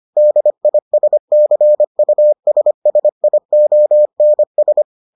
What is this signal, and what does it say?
Morse code, heard as a keyed tone, about 25 wpm, 600 Hz. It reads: DISCUSSIONS